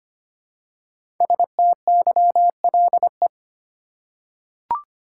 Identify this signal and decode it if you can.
Morse code, heard as a keyed tone, about 25 wpm, 700 Hz. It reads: STYLE
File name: 5 Letter Words - First 3 Letters Given - 25wpm